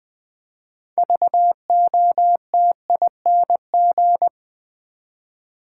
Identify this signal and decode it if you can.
Morse code, heard as a keyed tone, about 20 wpm, 700 Hz. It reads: VOTING
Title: 6 Letter Words - First 3 Letters Given - 20wpm